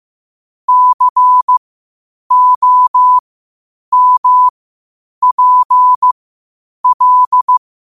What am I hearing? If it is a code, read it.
Morse code, heard as a keyed tone, about 15 wpm, spaced as for 8 wpm, 1000 Hz. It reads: COMPL
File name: 8wpm